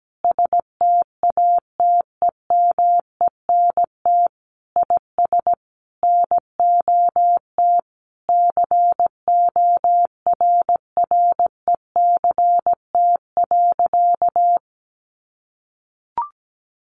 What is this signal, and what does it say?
Morse code, heard as a keyed tone, about 17 wpm, 700 Hz. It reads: STATEMENT IS NOT CORRECT.
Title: Sentences from Top 1200 Words 17wpm